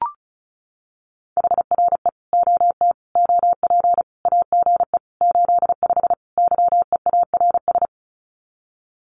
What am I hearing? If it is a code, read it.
Morse code, heard as a keyed tone, about 34 wpm, 700 Hz. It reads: HRE OT OP AGE 85 YEARS
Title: Common QSO Phrases 35wpm